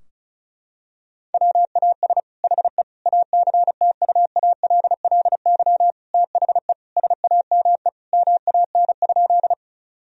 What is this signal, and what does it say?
Morse code, heard as a keyed tone, about 36 wpm, 700 Hz. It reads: WAS HE ACTUALLY THE SAME MAN?
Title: Sentences from Top 1000 Words - Review Part 4 - 35wpm